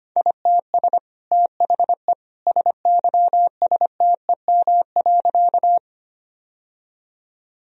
Morse code, heard as a keyed tone, about 25 wpm, 700 Hz. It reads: ITS THE SYSTEM.
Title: Sentences from Top 1000 Words - Review Part 3 - 25wpm